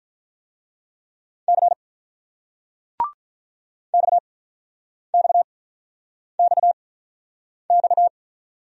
Morse code, heard as a keyed tone, about 42 wpm, spaced as for 9 wpm, 700 Hz. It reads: C XXXX
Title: Speed-Racing Single Letters 35wpm